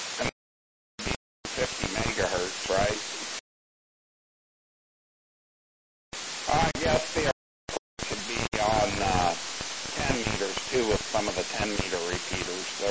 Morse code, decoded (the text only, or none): none